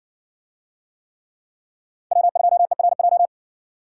Morse code, heard as a keyed tone, about 70 wpm, 700 Hz. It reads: W2ERJ